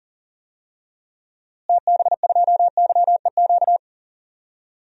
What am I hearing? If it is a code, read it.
Morse code, heard as a keyed tone, about 40 wpm, 700 Hz. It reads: TB2YEQ
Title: DXCC-100 Call Signs - 6 Characters - Repeated 2x - 40wpm